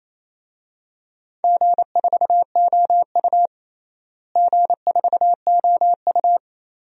Morse code, heard as a keyed tone, about 27 wpm, 700 Hz. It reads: G4OU G4OU
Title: DXCC-100 Call Signs - 4 Characters - Repeated 3x - 28wpm